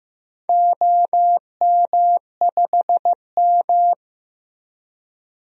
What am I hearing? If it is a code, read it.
Morse code, heard as a keyed tone, about 15 wpm, 700 Hz. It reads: OM5M
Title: DXCC-100 Call Signs - 4 Characters - Repeated 2x - 15wpm